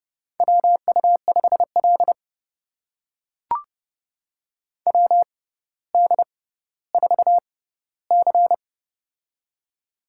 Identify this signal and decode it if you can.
Morse code, heard as a keyed tone, about 30 wpm, spaced as for 15 wpm, 700 Hz. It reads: WU5L WD4C
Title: USA Call Signs - 4 Characters - 6x Character Spacing - 30wpm